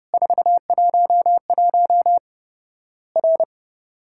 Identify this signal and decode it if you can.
Morse code, heard as a keyed tone, about 30 wpm, 700 Hz. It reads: 411 R
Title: Worked All Europe DX Contest - QTC Exchange - 30wpm